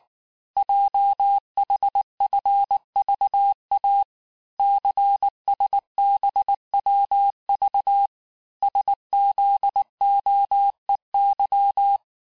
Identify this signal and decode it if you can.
Morse code, heard as a keyed tone, about 19 wpm, 800 Hz. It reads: JHFVA CSBWV SZOEY
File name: cours-du-lundi files / cours-1589-texte-chiffre-vitesse-18mpm